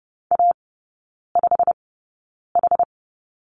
Morse code, heard as a keyed tone, about 30 wpm, spaced as for 9 wpm, 700 Hz. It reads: A5H
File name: course / 30:10wpm / 087 Lesson 15e - Number 5 - Three Characters 30wpm